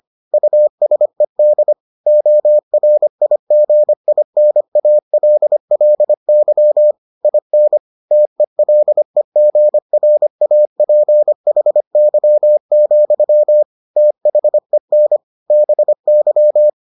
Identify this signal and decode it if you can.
Morse code, heard as a keyed tone, about 25 wpm, 600 Hz. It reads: USED ORIGINALLY IN TELEGRAPHY, THEN BY